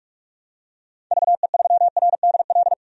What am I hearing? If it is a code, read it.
Morse code, heard as a keyed tone, about 46 wpm, 700 Hz. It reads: VE3RDL
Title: DXCC-100 Call Signs - 6 Characters - 45wpm